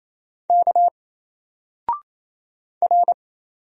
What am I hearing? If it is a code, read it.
Morse code, heard as a keyed tone, about 27 wpm, 700 Hz. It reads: K R
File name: Single-Letter-Number-28wpm